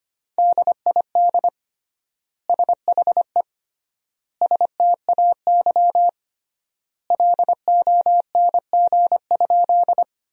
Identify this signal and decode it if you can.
Morse code, heard as a keyed tone, about 25 wpm, 700 Hz. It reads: DID SHE STAY LONG?
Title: 3x Word-Spacing - Sentences from Top 500 Words - 25wpm